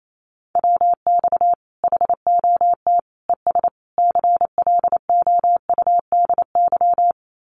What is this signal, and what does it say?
Morse code, heard as a keyed tone, about 28 wpm, 700 Hz. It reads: WX HOT ES CLOUDY
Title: Common QSO Phrases 28wpm